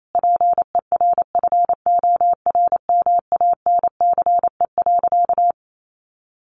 Morse code, heard as a keyed tone, about 28 wpm, 700 Hz. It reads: PERFORMANCE.
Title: Sentences from Top 1100 Words 28wpm